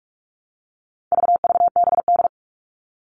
Morse code, heard as a keyed tone, about 44 wpm, 700 Hz. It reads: VVBD
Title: All Cut Numbers - 4-Digit 45wpm